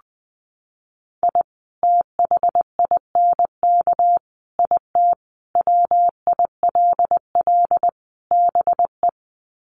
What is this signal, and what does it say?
Morse code, heard as a keyed tone, about 20 wpm, 700 Hz. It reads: I THINK IT WILL BE
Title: Sets of 5 Words Encore2 20wpm